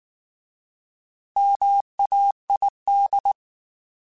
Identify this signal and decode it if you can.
Morse code, heard as a keyed tone, about 19 wpm, 800 Hz. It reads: MAID